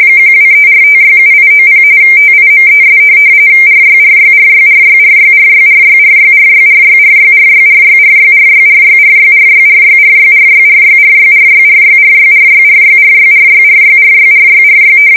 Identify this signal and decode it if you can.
RTTY, about 45.45 baud. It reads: 1234567890 TIMES RYRYRYRYRYRYRYRYRYRYRYRYRYRYRYRYRYRYRYRYRYRYRYRYRYRYRYRYRYRYRYRYRYRY